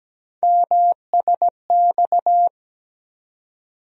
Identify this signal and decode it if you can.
Morse code, heard as a keyed tone, about 17 wpm, 700 Hz. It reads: MSX
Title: USA Call Sign Postfixes - 3 Letters 17wpm